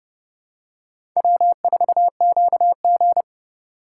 Morse code, heard as a keyed tone, about 31 wpm, 700 Hz. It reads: W4QG